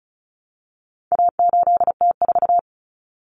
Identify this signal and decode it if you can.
Morse code, heard as a keyed tone, about 34 wpm, 700 Hz. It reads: A8T4
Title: Common Cut Numbers - 4-Digit 35wpm